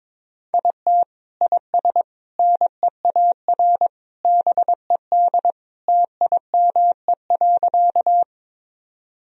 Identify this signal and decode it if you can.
Morse code, heard as a keyed tone, about 22 wpm, 700 Hz. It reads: IT IS NEAR BED TIME.
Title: Sentences from Top 1000 Words - Review Part 1 - 22wpm